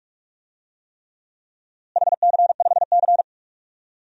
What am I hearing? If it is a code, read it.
Morse code, heard as a keyed tone, about 46 wpm, 700 Hz. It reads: HC5C
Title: DXCC-100 Call Signs - 4 Characters - Repeated 3x - 45wpm